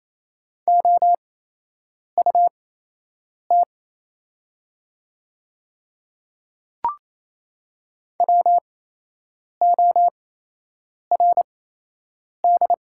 Morse code, heard as a keyed tone, about 28 wpm, spaced as for 8 wpm, 700 Hz. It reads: OUT WORD
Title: 8x Character Spacing - Top 100 Words - 28wpm